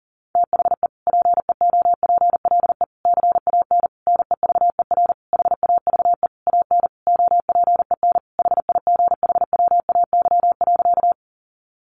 Morse code, heard as a keyed tone, about 40 wpm, 700 Hz. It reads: THE PEOPLE CAN NEVER HAVE AN OPEN HIGHWAY.
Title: Sentences from Top 1400 Words 40wpm